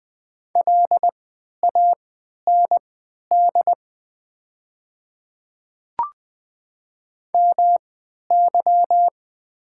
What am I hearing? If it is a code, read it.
Morse code, heard as a keyed tone, about 20 wpm, spaced as for 11 wpm, 690 Hz. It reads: LAND MY
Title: 3x Character Spacing - Top 300 Words - 20wpm